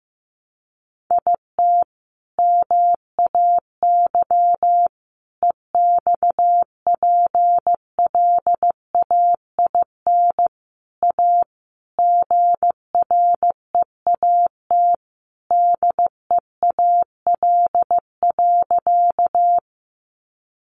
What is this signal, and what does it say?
Morse code, heard as a keyed tone, about 15 wpm, 700 Hz. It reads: IT MAY EXPLAIN A GREAT DEAL.
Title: Sentences from Top 1100 Words 15wpm